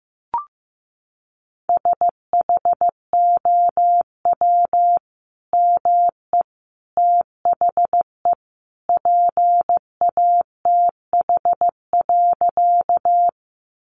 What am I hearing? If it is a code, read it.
Morse code, heard as a keyed tone, about 15 wpm, 700 Hz. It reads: SHOW ME THE PATH.
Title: Sentences from Top 1400 Words 15wpm